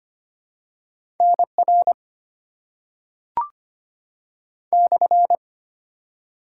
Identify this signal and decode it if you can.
Morse code, heard as a keyed tone, about 25 wpm, 700 Hz. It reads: NR /
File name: Top 90 QSO Elements - 6x Character Spacing - 25wpm